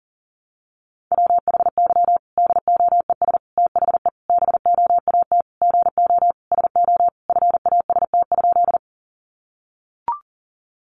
Morse code, heard as a keyed tone, about 40 wpm, 700 Hz. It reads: WHY DOES THE BOAT GO SO FAST?
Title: Sentences from Top 500 Words 40wpm